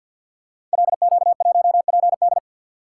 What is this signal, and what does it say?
Morse code, heard as a keyed tone, about 52 wpm, 700 Hz. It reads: LQ1PD
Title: Code Groups - 5 Letter:Numbers - 50wpm